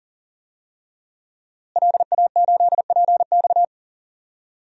Morse code, heard as a keyed tone, about 40 wpm, 700 Hz. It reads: LA8PX
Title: DXCC-100 Call Signs - 5 Characters - Repeated 3x - 40wpm